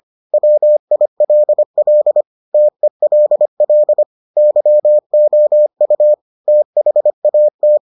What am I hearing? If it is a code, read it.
Morse code, heard as a keyed tone, about 25 wpm, 600 Hz. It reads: WILL TELL YOU THAT